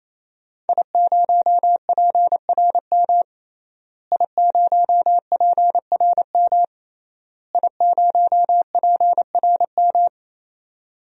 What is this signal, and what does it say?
Morse code, heard as a keyed tone, about 28 wpm, 700 Hz. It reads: I0PRM I0PRM I0PRM
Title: DXCC-100 Call Signs - 5 Characters - Repeated 3x - 28wpm